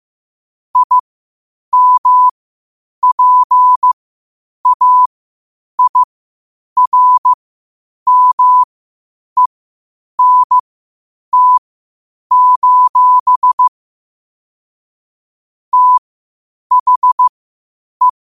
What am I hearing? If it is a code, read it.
Morse code, heard as a keyed tone, about 15 wpm, spaced as for 8 wpm, 1000 Hz. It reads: IMPAIRMENT: THE